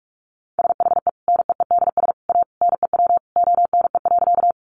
Morse code, heard as a keyed tone, about 44 wpm, 700 Hz. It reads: SHE NEEDS A NEW ONE.
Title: Additional Easy Phrases 45wpm